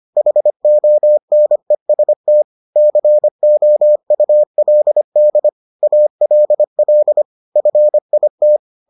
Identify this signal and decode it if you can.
Morse code, heard as a keyed tone, about 25 wpm, 600 Hz. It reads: HONEST COULD ALL FIT